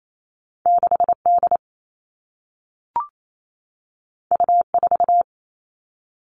Morse code, heard as a keyed tone, about 28 wpm, 700 Hz. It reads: 6 D U 4
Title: Two Challenging Letter-Number Combinations 28wpm